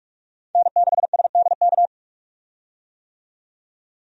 Morse code, heard as a keyed tone, about 45 wpm, 700 Hz. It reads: N6SDK
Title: USA Call Signs - 5 Characters - 45wpm